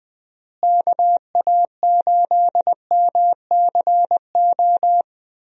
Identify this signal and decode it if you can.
Morse code, heard as a keyed tone, about 21 wpm, 700 Hz. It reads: KA8MCO
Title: USA Call Signs - 6 Characters - Repeated 3x - 20wpm